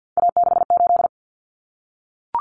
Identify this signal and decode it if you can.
Morse code, heard as a keyed tone, about 52 wpm, 700 Hz. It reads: A68
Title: Common Cut Numbers - 3-Digit 50wpm